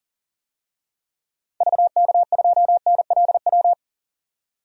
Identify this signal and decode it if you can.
Morse code, heard as a keyed tone, about 40 wpm, 700 Hz. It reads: VK2NLW
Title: DXCC-100 Call Signs - 6 Characters - 40wpm